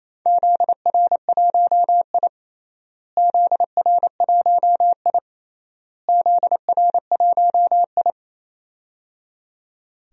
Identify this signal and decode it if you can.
Morse code, heard as a keyed tone, about 28 wpm, 700 Hz. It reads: ZR1I ZR1I ZR1I
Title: DXCC-100 Call Signs - 4 Characters - Repeated 3x - 28wpm